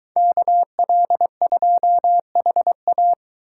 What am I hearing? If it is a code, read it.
Morse code, heard as a keyed tone, about 23 wpm, 700 Hz. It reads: KL2HA